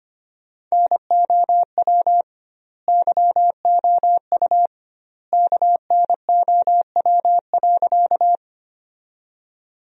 Morse code, heard as a keyed tone, about 25 wpm, 700 Hz. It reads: NOW YOU KNOW.
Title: 2x Word-Spacing - Sentences from Top 100 Words - 25wpm